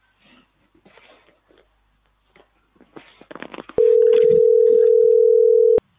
Ringback tone (North America), 440 + 480 Hz.